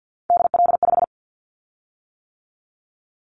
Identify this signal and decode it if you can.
Morse code, heard as a keyed tone, about 50 wpm, 700 Hz. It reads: DL5
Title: Callsign Prefixes for Top 50 DXCC - 50wpm